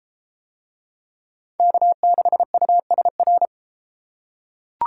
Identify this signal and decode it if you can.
Morse code, heard as a keyed tone, about 32 wpm, 700 Hz. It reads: K6USR